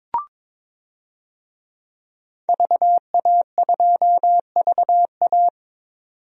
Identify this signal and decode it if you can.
Morse code, heard as a keyed tone, about 22 wpm, 700 Hz. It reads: VA2VA
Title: DXCC-100 Call Signs - 5 Characters - 22wpm